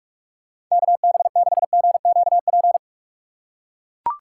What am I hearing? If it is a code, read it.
Morse code, heard as a keyed tone, about 42 wpm, 700 Hz. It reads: KB6GQP
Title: USA Call Signs - 6 Characters - 2x Character Spacing - 45wpm